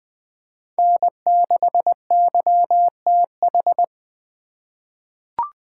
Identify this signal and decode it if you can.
Morse code, heard as a keyed tone, about 20 wpm, 700 Hz. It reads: N6YTH
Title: USA Call Signs - 4x Character Spacing - 20wpm